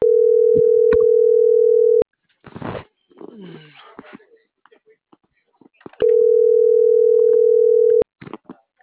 Ringback tone (North America), 440 + 480 Hz.
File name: CENTRAL MAERSK / out luis.herrera